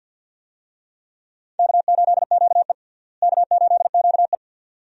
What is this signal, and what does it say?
Morse code, heard as a keyed tone, about 47 wpm, 700 Hz. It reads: K8QE K8QE
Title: USA Call Signs - 4 Characters - Repeated 2x - 50wpm